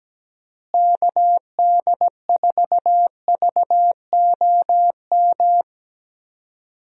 Morse code, heard as a keyed tone, about 17 wpm, 700 Hz. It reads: KD4VOM
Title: USA Call Signs - 6 Characters - Repeated 3x - 17wpm